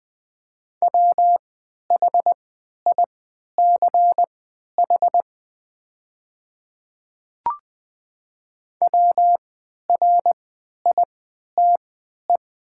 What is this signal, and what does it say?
Morse code, heard as a keyed tone, about 20 wpm, spaced as for 11 wpm, 700 Hz. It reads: WHICH WRITE
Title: 3x Character Spacing - Top 100 Words - 20wpm